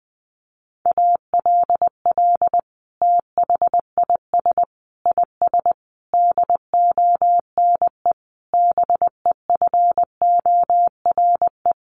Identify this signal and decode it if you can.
Morse code, heard as a keyed tone, about 20 wpm, 700 Hz. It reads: ALL THIS IS DONE BEFORE
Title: Sentences from Top 1500 Words 20wpm